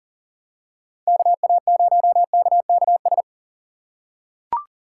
Morse code, heard as a keyed tone, about 40 wpm, 700 Hz. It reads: KA0KKS
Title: USA Call Signs - 6 Characters - 4x Character Spacing - 40wpm